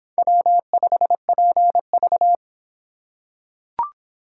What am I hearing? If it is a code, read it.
Morse code, heard as a keyed tone, about 26 wpm, 700 Hz. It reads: W5PV